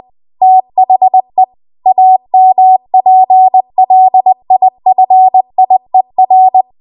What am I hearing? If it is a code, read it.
Morse code, heard as a keyed tone, about 20 wpm, 750 Hz. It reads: THE AMPLIFIER
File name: Archive / 20 WPM / 20WPM